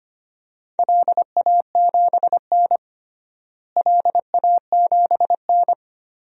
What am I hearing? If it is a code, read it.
Morse code, heard as a keyed tone, about 25 wpm, 700 Hz. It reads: LA7N LA7N
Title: DXCC-100 Call Signs - 4 Characters - Repeated 3x - 25wpm